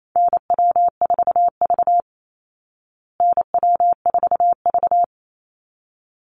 Morse code, heard as a keyed tone, about 28 wpm, 700 Hz. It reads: NW4V NW4V
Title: Call Signs Repeated 3x - 28wpm